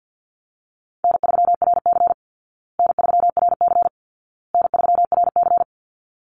Morse code, heard as a keyed tone, about 50 wpm, 700 Hz. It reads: N3RC N3RC N3RC
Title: Call Signs Repeated 2x - 50wpm